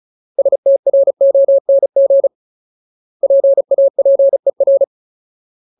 Morse code, heard as a keyed tone, about 34 wpm, 550 Hz. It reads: STRONG PAPER
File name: Ultimate Word List / 35WPM Ultimate